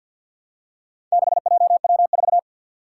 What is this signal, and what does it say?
Morse code, heard as a keyed tone, about 52 wpm, 700 Hz. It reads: 6JW4
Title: Four Challenging Letter-Number Combinations 50wpm